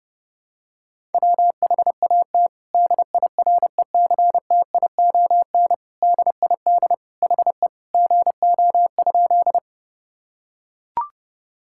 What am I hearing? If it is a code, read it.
Morse code, heard as a keyed tone, about 32 wpm, 700 Hz. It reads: WHAT DIRECTION DID HE GO?
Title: Sentences from Top 1000 Words - Review Part 5 - 30wpm